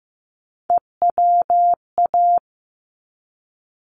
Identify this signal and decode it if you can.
Morse code, heard as a keyed tone, about 15 wpm, 700 Hz. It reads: EWA
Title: ARRL Sections 15wpm